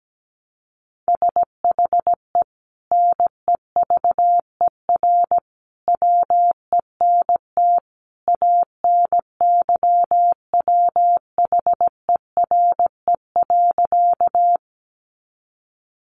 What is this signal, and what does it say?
Morse code, heard as a keyed tone, about 17 wpm, 700 Hz. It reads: SHE NEVER WENT ANYWHERE.
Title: Sentences from Top 1400 Words 17wpm